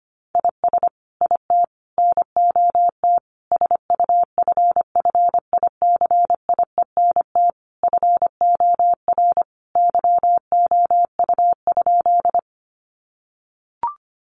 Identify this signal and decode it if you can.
Morse code, heard as a keyed tone, about 25 wpm, 700 Hz. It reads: IS IT NOT SUFFICIENT FOR YOU?
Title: Sentences from Top 1400 Words 25wpm